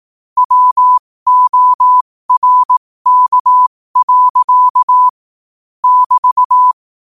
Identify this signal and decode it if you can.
Morse code, heard as a keyed tone, about 18 wpm, spaced as for 15 wpm, 1000 Hz. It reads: WORK. <BT>
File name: qotdfiles / 15wpm